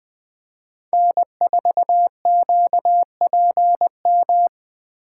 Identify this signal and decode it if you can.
Morse code, heard as a keyed tone, about 20 wpm, 700 Hz. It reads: N4QPM
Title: Speed Racing - USA 1x3 Call Signs 17wpm